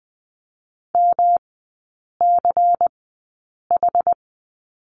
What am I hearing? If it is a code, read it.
Morse code, heard as a keyed tone, about 20 wpm, spaced as for 8 wpm, 700 Hz. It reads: MCH
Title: 133 Lesson 22e - Question Mark - Three Characters 20wpm